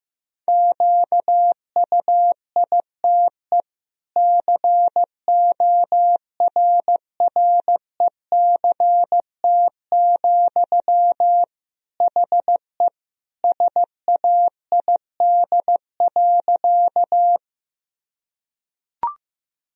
Morse code, heard as a keyed tone, about 15 wpm, 700 Hz. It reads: QUITE CORRECT, HE SAID.
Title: Sentences from Top 1000 Words - Review Part 2 - 15wpm